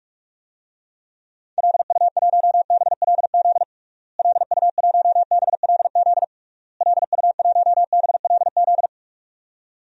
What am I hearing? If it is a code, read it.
Morse code, heard as a keyed tone, about 41 wpm, 700 Hz. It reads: LU1BLZ LU1BLZ LU1BLZ
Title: DXCC-100 Call Signs - 6 Characters - Repeated 3x - 45wpm